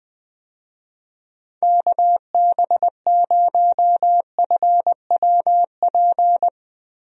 Morse code, heard as a keyed tone, about 20 wpm, 700 Hz. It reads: KB0FWP
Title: USA Call Signs - 6 Characters - 20wpm